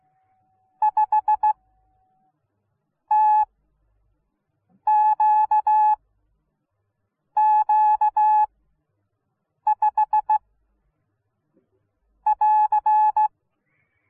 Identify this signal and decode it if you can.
Morse code, heard as a keyed tone, about 16 wpm, spaced as for 5 wpm, 850 Hz. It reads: 5TQQ5 <AR>